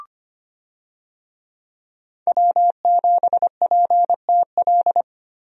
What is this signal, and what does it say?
Morse code, heard as a keyed tone, about 25 wpm, 690 Hz. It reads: W7PTL
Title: USA Call Signs - 5 Characters - 25wpm